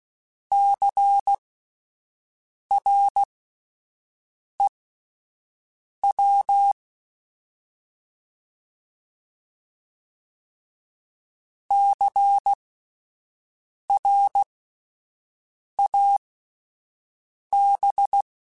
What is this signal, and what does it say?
Morse code, heard as a keyed tone, about 16 wpm, spaced as for 5 wpm, 800 Hz. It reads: CREW CRAB
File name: Spaced Alphas / Spaced Alphas mp3s / Cs sp @10